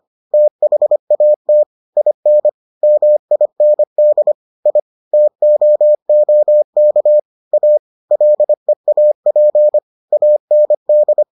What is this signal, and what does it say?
Morse code, heard as a keyed tone, about 25 wpm, 600 Hz. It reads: THAT IN MIND I TOOK A LEAP AND